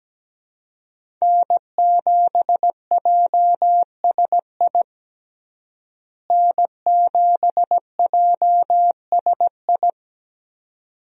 Morse code, heard as a keyed tone, about 17 wpm, 700 Hz. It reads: N7JSI N7JSI
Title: USA Call Signs - 5 Characters - Repeated 2x - 17wpm